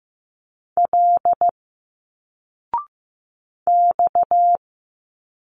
Morse code, heard as a keyed tone, about 15 wpm, 700 Hz. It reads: L X
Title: Single Letters 15wpm